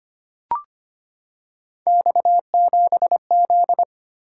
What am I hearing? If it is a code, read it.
Morse code, heard as a keyed tone, about 25 wpm, 700 Hz. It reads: X7Z
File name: Three Letter-Number Combinations - 25wpm